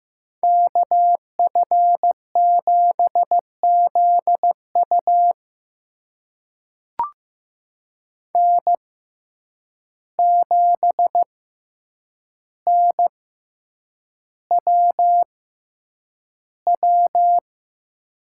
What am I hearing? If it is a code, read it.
Morse code, heard as a keyed tone, about 15 wpm, spaced as for 8 wpm, 700 Hz. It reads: KF7ZU N7NWW
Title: USA Call Signs - 5 Characters - 6x Character Spacing - 15wpm